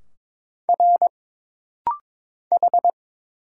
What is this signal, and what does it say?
Morse code, heard as a keyed tone, about 22 wpm, 700 Hz. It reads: R H
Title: Single Letters - Rapid-Fire 22wpm